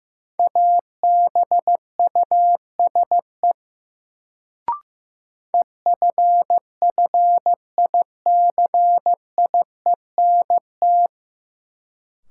Morse code, heard as a keyed tone, about 15 wpm, 700 Hz. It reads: ABUSE EFFICIENT
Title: Sentences from Top 1300 Words - New Word Drill Alt - 15wpm